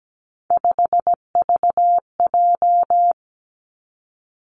Morse code, heard as a keyed tone, about 17 wpm, 700 Hz. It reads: 5VJ
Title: Three Challenging Letter-Number Combinations 17wpm